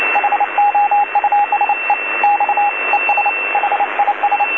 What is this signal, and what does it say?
Morse code, heard as a keyed tone, about 29 wpm, 820 Hz, with radio noise. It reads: HOUSE X ES HIS
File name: DC-TRCVR / 80-CW